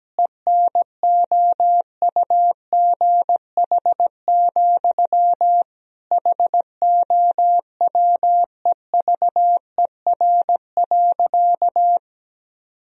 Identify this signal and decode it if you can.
Morse code, heard as a keyed tone, about 17 wpm, 700 Hz. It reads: ENOUGH, HOWEVER.
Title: Sentences from Top 1000 Words - Review Part 1 - 17wpm